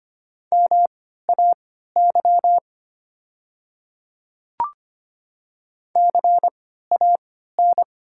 Morse code, heard as a keyed tone, about 25 wpm, spaced as for 14 wpm, 700 Hz. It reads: MAY CAN
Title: 3x Character Spacing - Top 100 Words - 25wpm